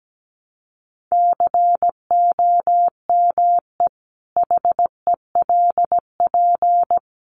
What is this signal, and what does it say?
Morse code, heard as a keyed tone, about 17 wpm, 700 Hz. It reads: COME HELP
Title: Sentences from Top 1300 Words 17wpm